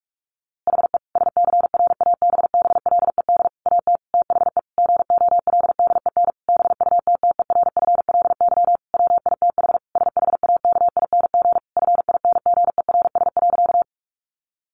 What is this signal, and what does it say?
Morse code, heard as a keyed tone, about 45 wpm, 700 Hz. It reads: HE SCRABBLED AT THE GOLDEN BUTTERFLY WITH SHAKING FINGERS.
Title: A Sound Of Thunder 45wpm